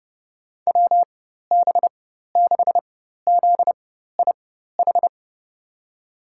Morse code, heard as a keyed tone, about 31 wpm, spaced as for 14 wpm, 700 Hz. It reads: WB6ZIH
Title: USA Call Signs - 6 Characters - 4x Character Spacing - 30wpm